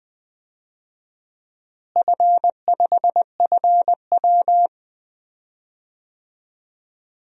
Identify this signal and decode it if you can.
Morse code, heard as a keyed tone, about 21 wpm, 700 Hz. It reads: F5FW
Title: DXCC-100 Call Signs - 4 Characters - Repeated 3x - 20wpm